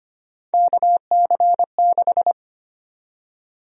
Morse code, heard as a keyed tone, about 25 wpm, 700 Hz. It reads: KC6
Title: USA Call Sign Prefixes - 25wpm